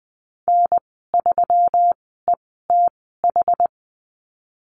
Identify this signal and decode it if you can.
Morse code, heard as a keyed tone, about 20 wpm, spaced as for 15 wpm, 700 Hz. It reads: N3ETH
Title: 173 Lesson 28g - Number 9 - Call Signs 20wpm